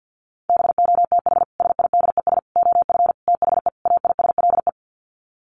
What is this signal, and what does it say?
Morse code, heard as a keyed tone, about 50 wpm, 700 Hz. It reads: BOTH SIDES OF THE AISLE
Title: Sets of 5 Words Encore2 50wpm